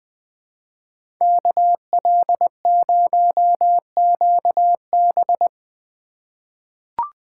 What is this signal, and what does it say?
Morse code, heard as a keyed tone, about 20 wpm, 700 Hz. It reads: KL0QB
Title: USA Call Signs - 5 Characters - 4x Character Spacing - 20wpm